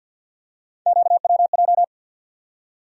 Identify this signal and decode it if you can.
Morse code, heard as a keyed tone, about 49 wpm, 700 Hz. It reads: QWJ
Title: USA Call Sign Postfixes - 3 Letters 50wpm